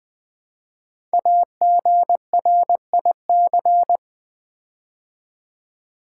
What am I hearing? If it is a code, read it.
Morse code, heard as a keyed tone, about 20 wpm, 700 Hz. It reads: AGRIC
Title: Build A Word - 5 Letters At A Time - 12-Letter Words - 20wpm